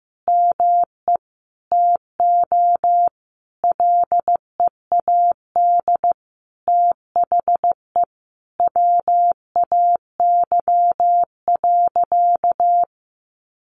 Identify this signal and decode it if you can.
Morse code, heard as a keyed tone, about 15 wpm, 700 Hz. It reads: ME TO LEAD THE WAY.
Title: Sentences from Top 700 Words 15wpm